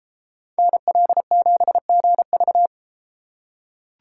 Morse code, heard as a keyed tone, about 32 wpm, 700 Hz. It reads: NL7GV